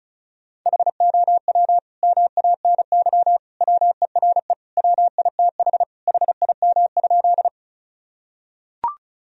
Morse code, heard as a keyed tone, about 34 wpm, 700 Hz. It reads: HOW MANY WERE WITH HIM?
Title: Sentences from Top 1000 Words - Review Part 1 - 35wpm